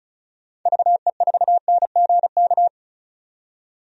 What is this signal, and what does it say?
Morse code, heard as a keyed tone, about 34 wpm, 700 Hz. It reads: VE4NGK
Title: DXCC-100 Call Signs - 6 Characters - 35wpm